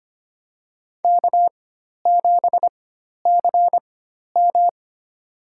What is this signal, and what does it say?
Morse code, heard as a keyed tone, about 25 wpm, spaced as for 12 wpm, 700 Hz. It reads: K7CM
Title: USA Call Signs - 4 Characters - 4x Character Spacing - 25wpm